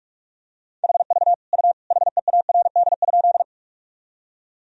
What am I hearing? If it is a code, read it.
Morse code, heard as a keyed tone, about 43 wpm, 700 Hz. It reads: HV U HEARD?
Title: Sentences from Top 400 Words and Abbreviations 45wpm